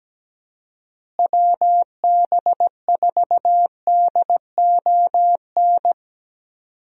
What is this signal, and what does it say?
Morse code, heard as a keyed tone, about 17 wpm, 700 Hz. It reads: WB4DON
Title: USA Call Signs - 6 Characters - Repeated 2x - 17wpm